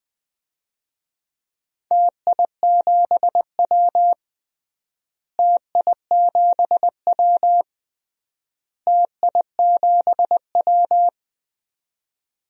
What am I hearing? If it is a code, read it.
Morse code, heard as a keyed tone, about 20 wpm, 700 Hz. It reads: TI7W TI7W TI7W
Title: DXCC-100 Call Signs - 4 Characters - Repeated 3x - 20wpm